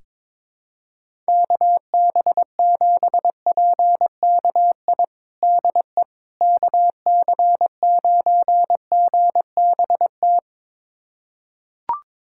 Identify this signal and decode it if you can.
Morse code, heard as a keyed tone, about 22 wpm, 700 Hz. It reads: KB7PKI DE KC9GBT
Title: USA Callsign DE Callsign - 6 Characters - 22wpm